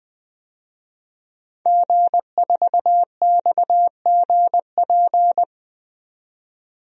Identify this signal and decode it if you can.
Morse code, heard as a keyed tone, about 20 wpm, 700 Hz. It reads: G4XGP
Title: DXCC-100 Call Signs - 5 Characters - 20wpm